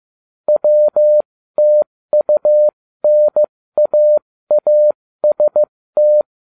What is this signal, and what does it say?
Morse code, heard as a keyed tone, about 15 wpm, spaced as for 13 wpm, 600 Hz. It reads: WTUNAAST